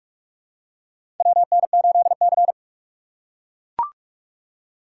Morse code, heard as a keyed tone, about 43 wpm, 700 Hz. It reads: WN8C
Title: USA Call Signs - 4 Characters - 6x Character Spacing - 45wpm